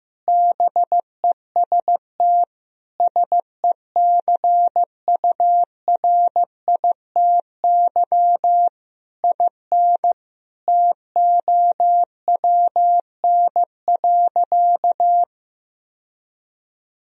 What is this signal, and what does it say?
Morse code, heard as a keyed tone, about 15 wpm, 700 Hz. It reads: BEST SECURITY IN TOWN.
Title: Sentences from Top 1000 Words - Review Part 4 - 15wpm